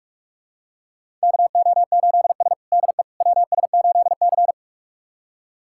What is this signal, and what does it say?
Morse code, heard as a keyed tone, about 41 wpm, 700 Hz. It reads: KO8S DE WS8C
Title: USA Callsign DE Callsign - 4 Characters - 45wpm